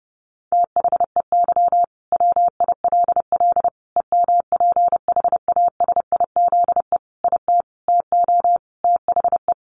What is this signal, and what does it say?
Morse code, heard as a keyed tone, about 30 wpm, 700 Hz. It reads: THEY WILL EMPHASIZE IT TO THE
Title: Sentences from Top 1500 Words 30wpm